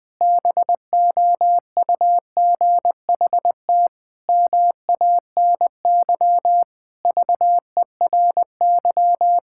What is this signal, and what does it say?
Morse code, heard as a keyed tone, about 20 wpm, 700 Hz. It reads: BOUGHT MANY VERY
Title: Pangram Challenge v2 - 20wpm